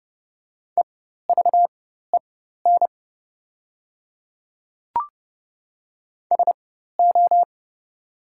Morse code, heard as a keyed tone, about 30 wpm, spaced as for 14 wpm, 700 Hz. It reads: EVEN SO
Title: 4x Character Spacing - Top 300 Words - 30wpm